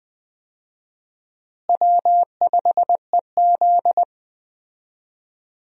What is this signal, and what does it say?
Morse code, heard as a keyed tone, about 20 wpm, 700 Hz. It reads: W5EZ
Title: USA Call Signs - 4 Characters - Repeated 3x - 20wpm